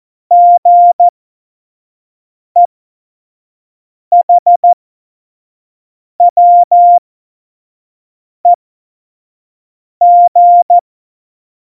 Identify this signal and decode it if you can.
Morse code, heard as a keyed tone, about 14 wpm, spaced as for 5 wpm, 700 Hz. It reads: GEHWEG
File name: Morsekurs / Witze / Witz 02 Juni 2020, 25 F70